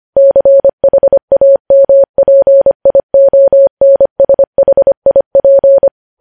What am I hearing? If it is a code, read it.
Morse code, heard as a keyed tone, about 25 wpm, 550 Hz. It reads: CHAMPIONSHIP
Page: news headlines in morse code at 25 wpm